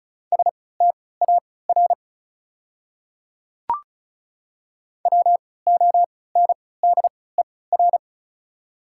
Morse code, heard as a keyed tone, about 34 wpm, spaced as for 20 wpm, 700 Hz. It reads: STAR WONDER
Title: 3x Character Spacing - Top 500 Words - 35wpm